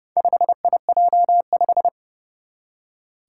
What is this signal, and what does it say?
Morse code, heard as a keyed tone, about 30 wpm, 700 Hz. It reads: 5IJ5
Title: Four Challenging Letter-Number Combinations 30wpm